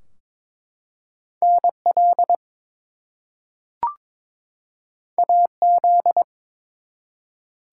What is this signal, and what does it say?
Morse code, heard as a keyed tone, about 22 wpm, 700 Hz. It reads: NL AZ
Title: US States and Canadian Provinces 22wpm